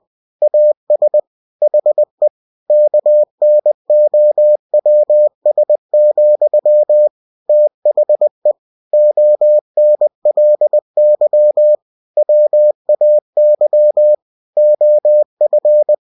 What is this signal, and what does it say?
Morse code, heard as a keyed tone, about 20 wpm, 600 Hz. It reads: AS HE KNOWS, THE ONLY WAY OF
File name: cw-books / winnie-the-pooh / winnie-20wpm-page-0000